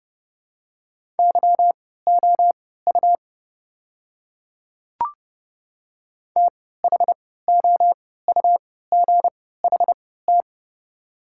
Morse code, heard as a keyed tone, about 30 wpm, spaced as for 17 wpm, 700 Hz. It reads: YOU THOUGHT
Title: 3x Character Spacing - Top 300 Words - 30wpm